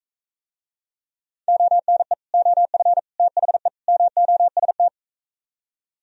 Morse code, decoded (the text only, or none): ONE OF THE MOST